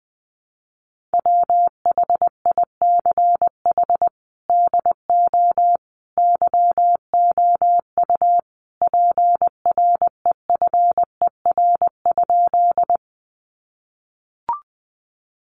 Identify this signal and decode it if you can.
Morse code, heard as a keyed tone, about 20 wpm, 700 Hz. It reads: WHICH DO YOU PREFER?
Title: Sentences from Top 1100 Words 20wpm